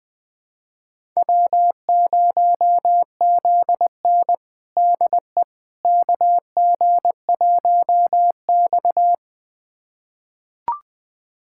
Morse code, heard as a keyed tone, about 21 wpm, 700 Hz. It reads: W0ZN DE KG1X
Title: USA Callsign DE Callsign - 4 Characters - 20wpm